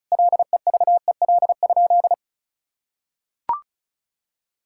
Morse code, heard as a keyed tone, about 36 wpm, 700 Hz. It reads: LEVEL?
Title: Sentences from Top 1000 Words - Review Part 5 - 35wpm